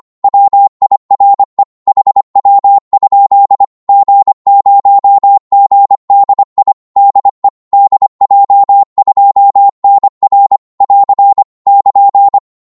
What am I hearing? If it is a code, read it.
Morse code, heard as a keyed tone, about 25 wpm, 800 Hz. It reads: WIRE HW? G0GDI DE DJ2NR <AR> <KN>
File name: Morse / qso-25wpm